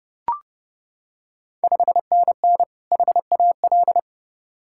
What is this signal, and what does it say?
Morse code, heard as a keyed tone, about 28 wpm, 690 Hz. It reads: 5NN HAL
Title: Ontario QSO Party Exchange 30wpm